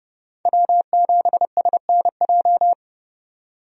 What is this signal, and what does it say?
Morse code, heard as a keyed tone, about 30 wpm, 700 Hz. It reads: W7SNJ
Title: USA Call Signs - 5 Characters - 30wpm